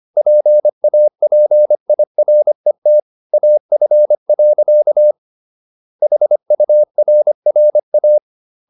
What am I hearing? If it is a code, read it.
Morse code, heard as a keyed tone, about 25 wpm, 600 Hz. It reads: PAPIRET AF. HURRA
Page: Ferie 25 wpm